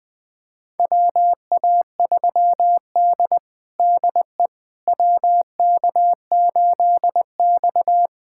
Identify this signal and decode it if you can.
Morse code, heard as a keyed tone, about 20 wpm, 700 Hz. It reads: WA3D DE WK8X
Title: USA Callsign DE Callsign - 4 Characters - 20wpm